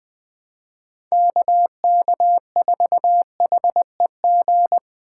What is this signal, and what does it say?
Morse code, heard as a keyed tone, about 20 wpm, 700 Hz. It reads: KK4HEG
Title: K1USN SST QSO Practice - 20wpm